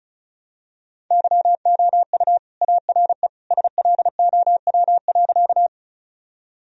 Morse code, heard as a keyed tone, about 34 wpm, 700 Hz. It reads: YOU ARE SLOW.
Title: Sentences from Top 1000 Words - Review Part 2 - 35wpm